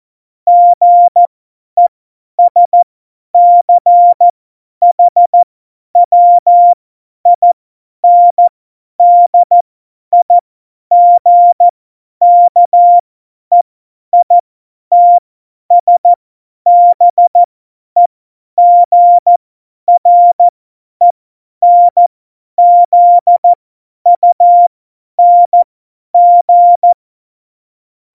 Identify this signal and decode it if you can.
Morse code, heard as a keyed tone, about 14 wpm, spaced as for 10 wpm, 700 Hz. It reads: GESCHWINDIGKEITSBEGRENZUNG
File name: Morsekurs / Witze / Witz 02 Juni 2020, 50 F70